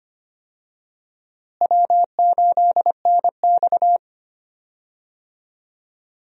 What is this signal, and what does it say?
Morse code, heard as a keyed tone, about 25 wpm, 700 Hz. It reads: W8NX
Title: USA Call Signs - 4 Characters - Repeated 3x - 25wpm